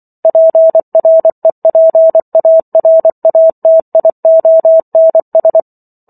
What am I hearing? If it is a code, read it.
Morse code, heard as a keyed tone, about 24 wpm, 650 Hz. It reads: PREPARATIONS